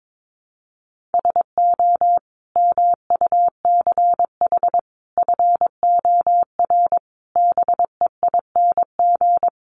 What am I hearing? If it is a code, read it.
Morse code, heard as a keyed tone, about 22 wpm, 700 Hz. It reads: SO MUCH FOR BEING
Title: Sets of 5 Words Encore 22wpm